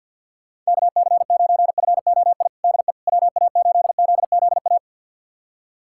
Morse code, heard as a keyed tone, about 52 wpm, 700 Hz. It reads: KC9FOI DE WA8ZZA
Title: USA Callsign DE Callsign - 6 Characters - 50wpm